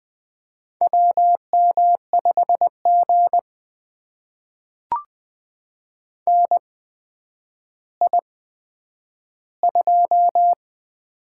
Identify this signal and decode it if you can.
Morse code, heard as a keyed tone, about 20 wpm, spaced as for 10 wpm, 700 Hz. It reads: WM5G NI2
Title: USA Call Signs - 4 Characters - 8x Character Spacing - 20wpm